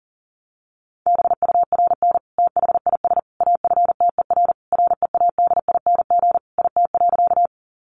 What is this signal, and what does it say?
Morse code, heard as a keyed tone, about 37 wpm, 700 Hz. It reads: BURN THIS AFTER READING IT.
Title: Sentences from Top 1300 Words 40wpm